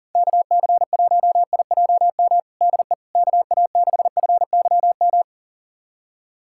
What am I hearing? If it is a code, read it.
Morse code, heard as a keyed tone, about 40 wpm, 700 Hz. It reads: KC1IJM DE KA6FYM
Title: USA Callsign DE Callsign - 6 Characters - 40wpm